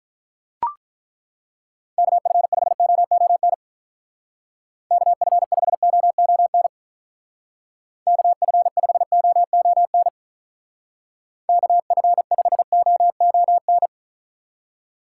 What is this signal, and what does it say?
Morse code, heard as a keyed tone, about 43 wpm, 700 Hz. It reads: KF5OON KF5OON KF5OON KF5OON
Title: Speed Racing - USA 2x3 Call Signs 35wpm